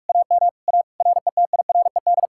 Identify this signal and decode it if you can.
Morse code, heard as a keyed tone, about 59 wpm, spaced as for 52 wpm, 690 Hz. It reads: AM A RETIRED
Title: Speed-Racing Common QSO Phrases - 45wpm